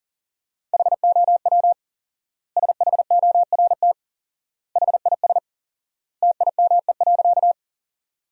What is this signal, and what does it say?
Morse code, heard as a keyed tone, about 40 wpm, 700 Hz. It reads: HOW SHORT HIS TIME.
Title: 4x Word-Spacing - Sentences from Top 400 Words - 40wpm